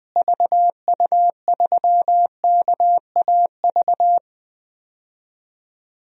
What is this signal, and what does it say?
Morse code, heard as a keyed tone, about 20 wpm, 700 Hz. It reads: VU3KAV
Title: DXCC-100 Call Signs - 6 Characters - Repeated 3x - 20wpm